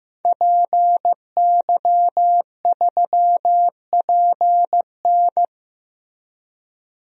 Morse code, heard as a keyed tone, about 15 wpm, 700 Hz. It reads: PY3PN
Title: DXCC-100 Call Signs - 5 Characters - Repeated 2x - 15wpm